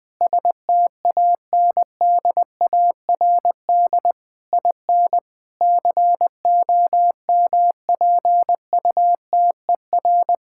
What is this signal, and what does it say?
Morse code, heard as a keyed tone, about 20 wpm, 700 Hz. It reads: STANDARD IN COMPUTER
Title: Sentences from Top 1000 Words - Review Part 6 - 20wpm